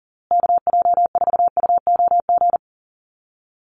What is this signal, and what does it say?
Morse code, heard as a keyed tone, about 40 wpm, 700 Hz. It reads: KJ4UOG
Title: USA Call Signs Repeated 3x 40wpm